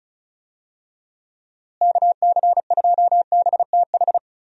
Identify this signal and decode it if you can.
Morse code, heard as a keyed tone, about 36 wpm, 700 Hz. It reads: KC2BTH
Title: USA Call Signs - 6 Characters - Repeated 2x - 35wpm